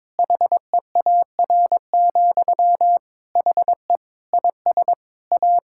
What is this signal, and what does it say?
Morse code, heard as a keyed tone, about 22 wpm, 700 Hz. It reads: HEAR, HE IS A
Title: Sentences from Top 1000 Words - Review Part 6 - 22wpm